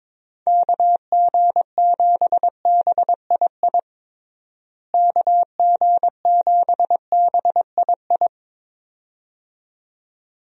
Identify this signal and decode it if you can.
Morse code, heard as a keyed tone, about 21 wpm, 700 Hz. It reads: KG7BII KG7BII
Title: USA Call Signs - 6 Characters - Repeated 2x - 22wpm